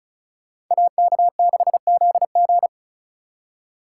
Morse code, heard as a keyed tone, about 46 wpm, 700 Hz. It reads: AK6ZG